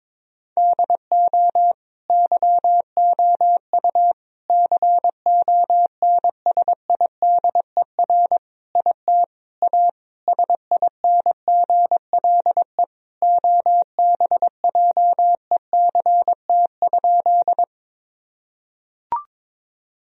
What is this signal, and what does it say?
Morse code, heard as a keyed tone, about 22 wpm, 700 Hz. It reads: DO YOU CONSIDER IT A SINGLE OBJECT?
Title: Sentences from Top 1000 Words - Review Part 4 - 22wpm